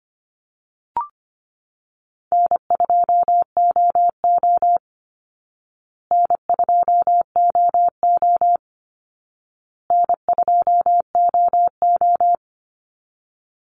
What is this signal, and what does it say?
Morse code, heard as a keyed tone, about 25 wpm, 700 Hz. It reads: N2OO N2OO N2OO
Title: Easy Call Signs Repeated 3x - 25wpm